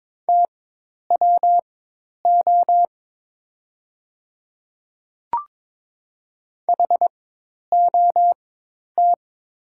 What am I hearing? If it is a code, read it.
Morse code, heard as a keyed tone, about 22 wpm, spaced as for 10 wpm, 690 Hz. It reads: TWO HOT
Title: 4x Character Spacing - Top 100 Words - 22wpm